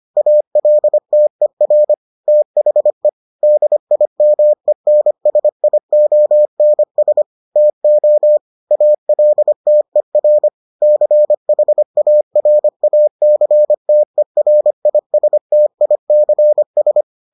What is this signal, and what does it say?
Morse code, heard as a keyed tone, about 25 wpm, 600 Hz. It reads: ALTER THE DIMENSIONS TO ALTER CHARACTERISTICS